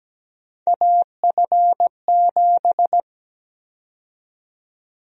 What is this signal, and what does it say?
Morse code, heard as a keyed tone, about 17 wpm, 700 Hz. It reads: AF7
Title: USA Call Sign Prefixes - 17wpm